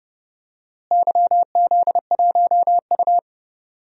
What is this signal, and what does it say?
Morse code, heard as a keyed tone, about 30 wpm, 700 Hz. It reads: YZ1U
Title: DXCC-100 Call Signs - 4 Characters - Repeated 2x - 30wpm